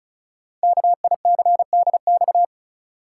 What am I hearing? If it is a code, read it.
Morse code, heard as a keyed tone, about 33 wpm, 700 Hz. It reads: KICDX
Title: Code Groups - 5 Letter:Numbers - 35wpm